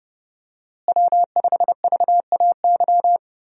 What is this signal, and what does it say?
Morse code, heard as a keyed tone, about 30 wpm, 700 Hz. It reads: W5VAY
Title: USA Call Signs - 5 Characters - Repeated 2x - 30wpm